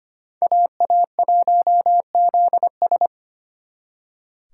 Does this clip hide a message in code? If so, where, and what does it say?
Yes: Morse code from 0.4 to 3.1 s, heard as a keyed tone, about 33 wpm, 700 Hz. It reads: AA1ZS